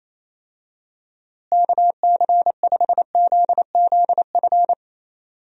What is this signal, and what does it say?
Morse code, heard as a keyed tone, about 28 wpm, 700 Hz. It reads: KC5ZZF
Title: USA Call Signs - 6 Characters - Repeated 3x - 28wpm